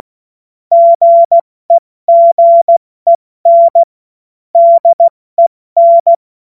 Morse code, heard as a keyed tone, about 16 wpm, spaced as for 14 wpm, 680 Hz. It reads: GEGEN DEN
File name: Morsekurs / Witze / Witz 17 Juni Mai 2025 70 F80